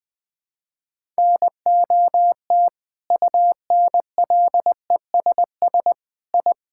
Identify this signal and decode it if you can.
Morse code, heard as a keyed tone, about 20 wpm, 700 Hz. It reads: NOT UNLESS I
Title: Sentences from Top 1000 Words - Review Part 6 - 20wpm